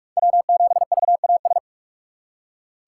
Morse code, heard as a keyed tone, about 44 wpm, 700 Hz. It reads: W7VAS
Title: K1USN SST QSO Practice - 45wpm